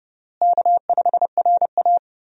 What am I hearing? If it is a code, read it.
Morse code, heard as a keyed tone, about 31 wpm, 700 Hz. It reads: K5RA
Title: USA Call Signs - 4 Characters - Repeated 2x - 30wpm